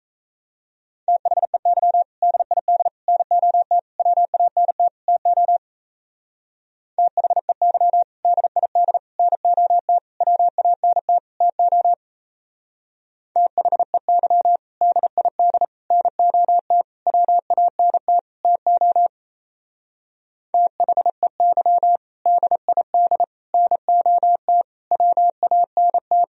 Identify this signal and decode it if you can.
Morse code, heard as a keyed tone, about 33 wpm, 700 Hz. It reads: THEY DID NOT WANT TO THEY DID NOT WANT TO THEY DID NOT WANT TO THEY DID NOT WANT
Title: Speed-Racing - Sets of 5 Words 28wpm